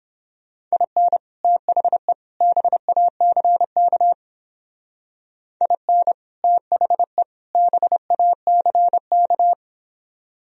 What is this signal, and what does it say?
Morse code, heard as a keyed tone, about 28 wpm, 700 Hz. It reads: IN THE BACK IN THE BACK